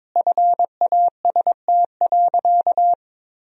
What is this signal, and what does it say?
Morse code, heard as a keyed tone, about 22 wpm, 700 Hz. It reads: FAST.
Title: Sentences from Top 1000 Words - Review Part 4 - 22wpm